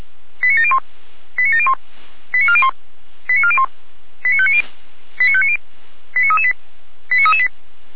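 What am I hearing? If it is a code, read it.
Five-tone selective call (ZVEI), heard as quick runs of five tones, tones of about 70 ms. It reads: 89701 89701 89401 89401 89590 89590 89338 89338
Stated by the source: ZVEI five-tone selective call 89356, received on an SDR.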